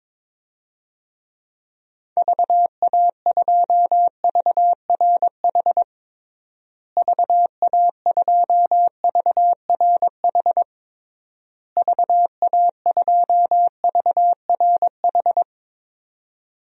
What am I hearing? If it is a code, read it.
Morse code, heard as a keyed tone, about 22 wpm, 700 Hz. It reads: VA2VRH VA2VRH VA2VRH
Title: DXCC-100 Call Signs - 6 Characters - Repeated 3x - 22wpm